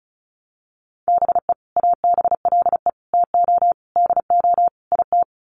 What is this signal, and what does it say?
Morse code, heard as a keyed tone, about 34 wpm, 700 Hz. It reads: BE ABLE TO DO IT
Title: Sets of 5 Words Encore2 35wpm